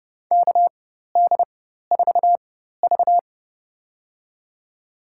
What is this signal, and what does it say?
Morse code, heard as a keyed tone, about 32 wpm, spaced as for 14 wpm, 700 Hz. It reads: KD4V
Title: USA Call Signs - 4 Characters - 4x Character Spacing - 30wpm